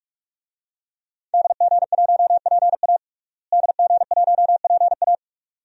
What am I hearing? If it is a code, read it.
Morse code, heard as a keyed tone, about 46 wpm, 700 Hz. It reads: DG1PA DG1PA
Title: DXCC-100 Call Signs - 5 Characters - Repeated 2x - 45wpm